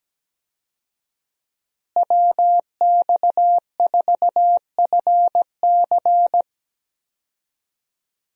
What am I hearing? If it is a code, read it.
Morse code, heard as a keyed tone, about 17 wpm, 700 Hz. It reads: WX4FC
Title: USA Call Signs - 5 Characters - Repeated 3x - 17wpm